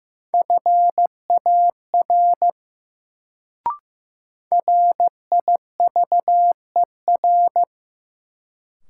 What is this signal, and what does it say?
Morse code, heard as a keyed tone, about 15 wpm, 700 Hz. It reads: FAR RIVER
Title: Sentences from Top 300 Words - New Word Drill - 15wpm